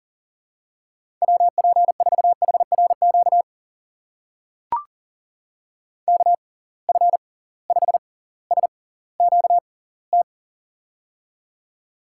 Morse code, heard as a keyed tone, about 40 wpm, spaced as for 20 wpm, 700 Hz. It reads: WP4HRQ KF5SQT
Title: USA Call Signs - 6 Characters - 6x Character Spacing - 40wpm